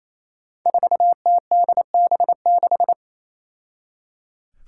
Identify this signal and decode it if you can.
Morse code, heard as a keyed tone, about 27 wpm, 700 Hz. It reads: 4TDB6
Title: Five Challenging Letter-Number Combinations 28wpm